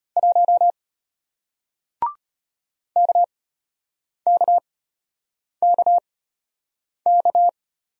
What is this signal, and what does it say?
Morse code, heard as a keyed tone, about 33 wpm, spaced as for 8 wpm, 700 Hz. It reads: 1 KKKK